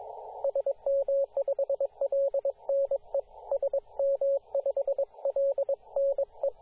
Morse code, heard as a keyed tone, about 22 wpm, 550 Hz, with radio noise. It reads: SM5LNE SM5LNE
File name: posts / 2017-sep-22-0800z-jd1bon-14-040mhz-cw